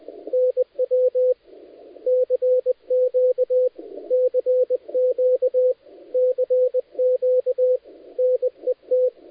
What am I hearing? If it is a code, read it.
Morse code, heard as a keyed tone, about 20 wpm, 510 Hz. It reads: NW CQ CQ CQ NET
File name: dlazaawansowanych / nagraniazpasma / titawaNETcontrol